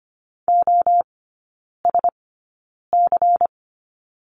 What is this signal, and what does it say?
Morse code, heard as a keyed tone, about 25 wpm, spaced as for 9 wpm, 700 Hz. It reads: OSC
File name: course / 25:10wpm / 093 Lesson 16e - Letter c - Three Characters 25wpm